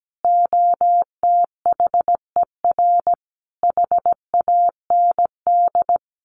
Morse code, heard as a keyed tone, about 17 wpm, 700 Hz. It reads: OTHER HAND
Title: Sets of 5 Words 17wpm